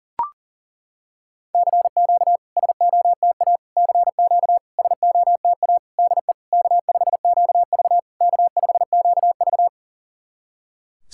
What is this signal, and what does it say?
Morse code, heard as a keyed tone, about 40 wpm, 700 Hz. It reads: CQ SOTA CQ SOTA DE K5QV K5QV
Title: SOTA QSO Practice 40wpm